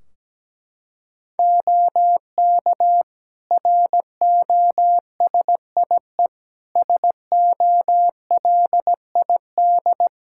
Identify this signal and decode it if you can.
Morse code, heard as a keyed tone, about 17 wpm, 700 Hz. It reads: OK ROSIE SOLID
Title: Practice QSO - Exchange 3 - More Info - 17wpm